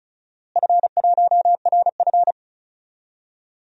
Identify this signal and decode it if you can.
Morse code, heard as a keyed tone, about 36 wpm, 700 Hz. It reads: F1RF
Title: DXCC-100 Call Signs - 4 Characters - Repeated 2x - 35wpm